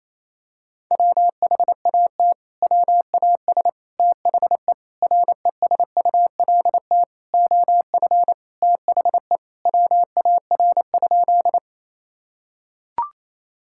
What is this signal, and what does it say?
Morse code, heard as a keyed tone, about 28 wpm, 700 Hz. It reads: WHAT WAS THE RESULT OF THE WAR?
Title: Sentences from Top 1000 Words - Review Part 5 - 28wpm